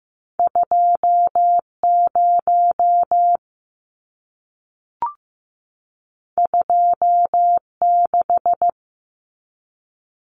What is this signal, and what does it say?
Morse code, heard as a keyed tone, about 15 wpm, 700 Hz. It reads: 2 0 2 6
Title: Numbers - 2-Digit 15wpm